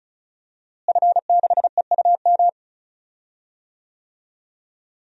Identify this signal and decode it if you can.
Morse code, heard as a keyed tone, about 36 wpm, 700 Hz. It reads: F6EUM
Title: DXCC-100 Call Signs - 5 Characters - Repeated 2x - 35wpm